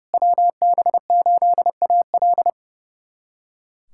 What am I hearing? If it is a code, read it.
Morse code, heard as a keyed tone, about 30 wpm, 700 Hz. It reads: WB8AL
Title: USA Callsign DE Callsign 30wpm